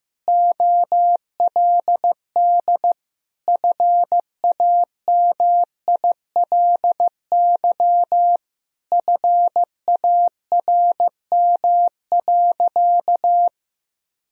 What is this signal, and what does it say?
Morse code, heard as a keyed tone, about 15 wpm, 700 Hz. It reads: OLD FAMILY FARM.
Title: Sentences from Top 1000 Words - Review Part 2 - 15wpm